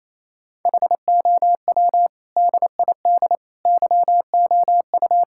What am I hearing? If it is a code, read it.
Morse code, heard as a keyed tone, about 28 wpm, 700 Hz. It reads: HOW DID YOU
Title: Sentences from Top 1000 Words - Review Part 5 - 28wpm